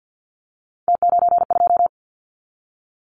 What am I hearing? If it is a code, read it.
Morse code, heard as a keyed tone, about 50 wpm, 700 Hz. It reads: T92
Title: Callsign Prefixes for Top 50 DXCC - 50wpm